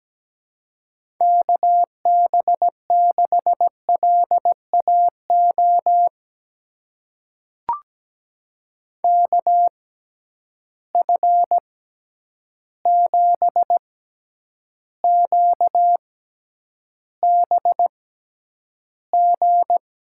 Morse code, heard as a keyed tone, about 17 wpm, spaced as for 9 wpm, 700 Hz. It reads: KB6LAO KF7QBG
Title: USA Call Signs - 6 Characters - 6x Character Spacing - 17wpm